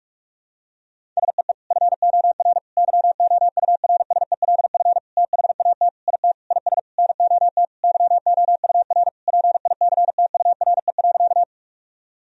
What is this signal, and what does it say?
Morse code, heard as a keyed tone, about 46 wpm, 700 Hz. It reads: SEE FOR YOURSELF THAT IT IS NOT YOUR PICTURE.
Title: Sentences from Top 1000 Words - Review Part 1 - 45wpm